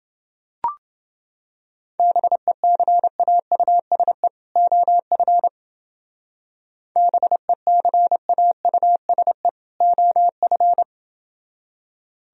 Morse code, heard as a keyed tone, about 29 wpm, 700 Hz. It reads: BECAUSE OF BECAUSE OF